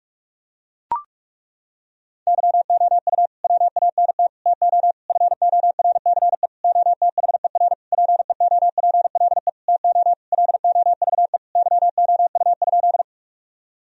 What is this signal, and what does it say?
Morse code, heard as a keyed tone, about 42 wpm, 700 Hz. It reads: YOU WANT TO FORCE OTHER PEOPLE TO LOVE YOU?
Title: Sentences from Top 1000 Words - Review Part 3 - 45wpm